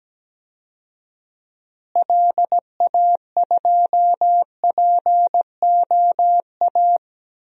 Morse code, heard as a keyed tone, about 17 wpm, 700 Hz. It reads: LA2POA
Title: DXCC-100 Call Signs - 6 Characters - Repeated 2x - 17wpm